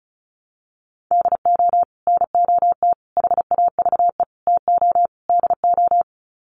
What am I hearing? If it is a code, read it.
Morse code, heard as a keyed tone, about 34 wpm, 700 Hz. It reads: DO NOT HAVE TO DO
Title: Sets of 5 Words 35wpm